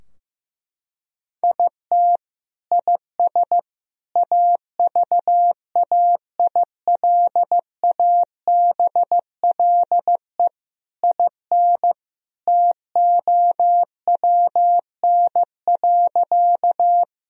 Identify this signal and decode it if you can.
Morse code, heard as a keyed tone, about 16 wpm, 700 Hz. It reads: IT IS AVAILABLE IN TOWN.
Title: Sentences from Top 1000 Words - Review Part 3 - 15wpm